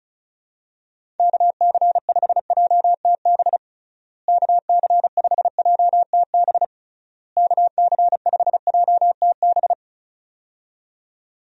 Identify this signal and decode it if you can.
Morse code, heard as a keyed tone, about 34 wpm, 700 Hz. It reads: KC5JTB KC5JTB KC5JTB
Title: USA Call Signs - 6 Characters - Repeated 3x - 35wpm